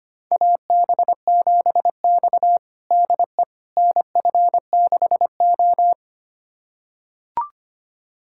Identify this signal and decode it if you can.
Morse code, heard as a keyed tone, about 25 wpm, 700 Hz. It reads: AB7X DE NF6O
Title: USA Callsign DE Callsign - 4 Characters - 25wpm